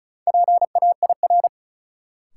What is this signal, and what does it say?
Morse code, heard as a keyed tone, about 36 wpm, 700 Hz. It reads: PAIR
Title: Sentences from Top 1200 Words - New Word Drill Alt - 35wpm